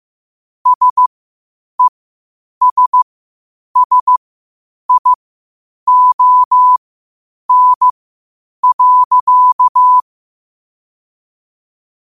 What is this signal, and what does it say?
Morse code, heard as a keyed tone, about 15 wpm, spaced as for 8 wpm, 1000 Hz. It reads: SESSION.
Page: New quotes every day in morse code at 8 Words per minute.